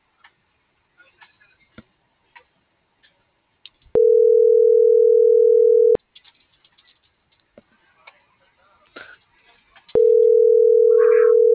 Ringback tone (North America), 440 + 480 Hz.